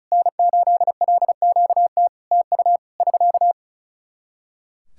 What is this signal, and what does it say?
Morse code, heard as a keyed tone, about 36 wpm, 700 Hz. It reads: N8LQT TU <SK>
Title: Practice QSO - Exchange 4 - Ending - 35wpm